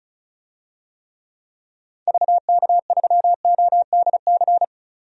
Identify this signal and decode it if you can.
Morse code, heard as a keyed tone, about 34 wpm, 700 Hz. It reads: VK3ODC
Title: DXCC-100 Call Signs - 6 Characters - 35wpm